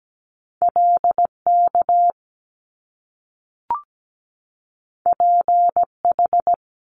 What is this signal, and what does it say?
Morse code, heard as a keyed tone, about 17 wpm, 700 Hz. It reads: LK PH
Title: Top Two Letter Combinations in Words 17wpm